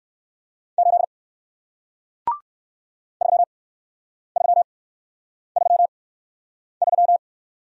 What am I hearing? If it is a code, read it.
Morse code, heard as a keyed tone, about 58 wpm, spaced as for 9 wpm, 700 Hz. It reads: 8 3333